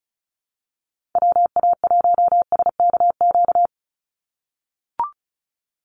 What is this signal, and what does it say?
Morse code, heard as a keyed tone, about 34 wpm, 700 Hz. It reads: WA1SKQ
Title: Call Signs - 35wpm